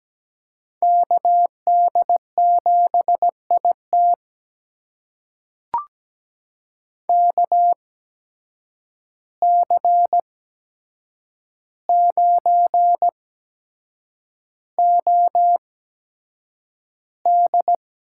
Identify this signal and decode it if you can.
Morse code, heard as a keyed tone, about 17 wpm, spaced as for 7 wpm, 700 Hz. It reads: KD7IT KC9OD
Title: USA Call Signs - 5 Characters - 8x Character Spacing - 17wpm